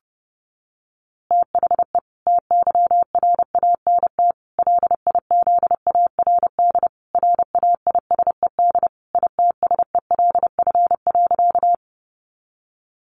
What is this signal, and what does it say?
Morse code, heard as a keyed tone, about 30 wpm, 700 Hz. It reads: THE TYRANT LIZARD RAISED ITSELF.
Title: A Sound Of Thunder 30wpm